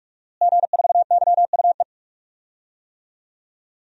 Morse code, heard as a keyed tone, about 46 wpm, 700 Hz. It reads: G4YUE
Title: DXCC-100 Call Signs - 5 Characters - Repeated 2x - 45wpm